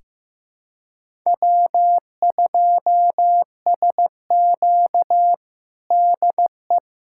Morse code, heard as a keyed tone, about 15 wpm, 700 Hz. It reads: W2SQ DE
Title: USA Callsign DE Callsign - 4 Characters - 15wpm